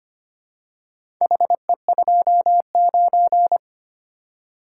Morse code, heard as a keyed tone, about 25 wpm, 700 Hz. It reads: HE29
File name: practice / 25wpm / Four Letter-Number Combinations - 25wpm